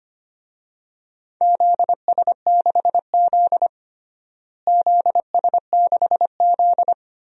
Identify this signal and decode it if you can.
Morse code, heard as a keyed tone, about 25 wpm, 700 Hz. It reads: ZS6Z ZS6Z
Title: DXCC-100 Call Signs - 4 Characters - Repeated 2x - 25wpm